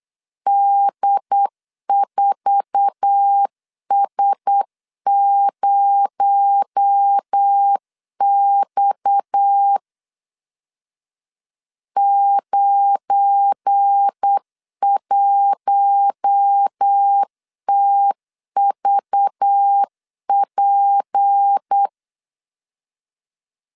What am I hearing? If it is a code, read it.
Morse code, heard as a keyed tone, about 8 wpm, 800 Hz. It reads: D4S0X 91TVP
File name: LEZIONI / 8 YJ45 / 8 YJ4 8wpm 3246